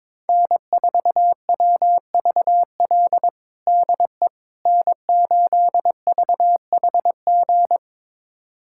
Morse code, heard as a keyed tone, about 22 wpm, 700 Hz. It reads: N4WVL DE N8VHG
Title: USA Callsign DE Callsign - 5 Characters - 22wpm